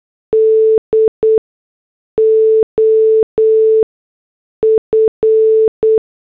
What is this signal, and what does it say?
Morse code, heard as a keyed tone, about 8 wpm, spaced as for 6 wpm, 440 Hz. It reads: D OF